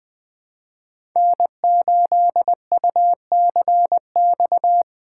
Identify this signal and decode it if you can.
Morse code, heard as a keyed tone, about 20 wpm, 700 Hz. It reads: N8UCX
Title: USA Call Signs - 5 Characters - 3x Character Spacing - 20wpm